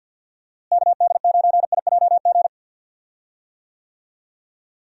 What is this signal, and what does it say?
Morse code, heard as a keyed tone, about 48 wpm, 700 Hz. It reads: KD9IJG
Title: USA Call Signs - 6 Characters - Repeated 3x - 50wpm